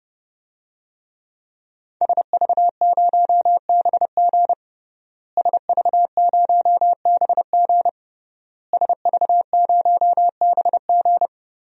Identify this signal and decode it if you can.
Morse code, heard as a keyed tone, about 32 wpm, 700 Hz. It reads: SV0BG SV0BG SV0BG
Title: DXCC-100 Call Signs - 5 Characters - Repeated 3x - 30wpm